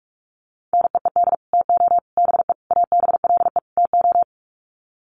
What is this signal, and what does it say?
Morse code, heard as a keyed tone, about 44 wpm, 700 Hz. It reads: NEED TO BE ABLE TO
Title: Sets of 5 Words Encore 45wpm